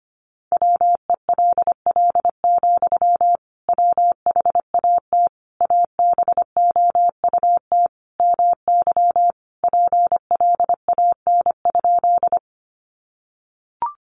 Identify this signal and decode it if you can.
Morse code, heard as a keyed tone, about 25 wpm, 700 Hz. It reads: WELL, WHAT ABOUT MY PLAN?
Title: Sentences from Top 500 Words 25wpm